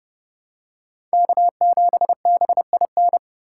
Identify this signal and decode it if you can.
Morse code, heard as a keyed tone, about 28 wpm, 700 Hz. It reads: K7BIN